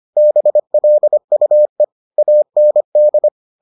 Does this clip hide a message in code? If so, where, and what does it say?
Yes: Morse code from 0.2 to 3.3 s, heard as a keyed tone, about 25 wpm, 600 Hz. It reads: BLUE AND